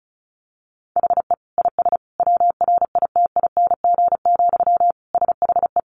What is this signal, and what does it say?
Morse code, heard as a keyed tone, about 34 wpm, 700 Hz. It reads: HE IS WRITING, SHE
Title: Sentences from Top 800 Words 35wpm